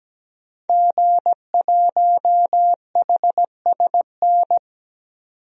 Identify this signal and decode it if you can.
Morse code, heard as a keyed tone, about 17 wpm, 700 Hz. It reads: G1HSN
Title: DXCC-100 Call Signs - 5 Characters - 17wpm